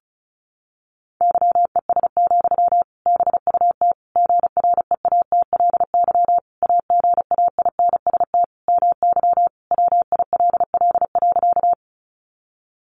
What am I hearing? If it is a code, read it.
Morse code, heard as a keyed tone, about 34 wpm, 700 Hz. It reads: YES, BUT GREATLY AGAINST MY WILL.
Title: Sentences from Top 1400 Words 35wpm